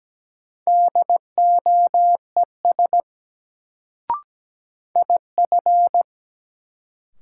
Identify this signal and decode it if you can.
Morse code, heard as a keyed tone, about 17 wpm, 700 Hz. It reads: DOES IF
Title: Sets of 2 Words - New Word Drill - 17wpm